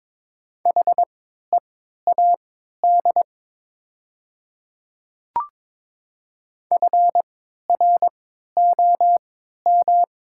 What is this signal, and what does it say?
Morse code, heard as a keyed tone, about 22 wpm, spaced as for 12 wpm, 700 Hz. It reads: HEAD FROM
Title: 3x Character Spacing - Top 300 Words - 22wpm